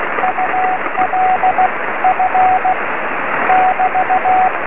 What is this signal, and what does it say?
Morse code, heard as a keyed tone, about 16 wpm, 740 Hz, with radio noise. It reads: UL F <BT>
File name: sm7lcb / 20 23cm / 2308 oe5vrl-5 23cm